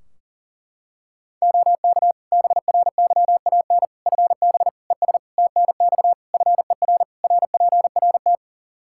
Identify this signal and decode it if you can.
Morse code, heard as a keyed tone, about 40 wpm, 700 Hz. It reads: OK BRYAN FB ES TNX FER RPRT
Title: Practice QSO - Exchange 2 - Info - 40wpm